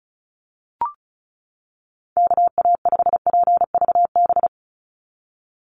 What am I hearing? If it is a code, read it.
Morse code, heard as a keyed tone, about 34 wpm, 700 Hz. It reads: KA5PVB
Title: NA SOTA Chaser Callsigns 35wpm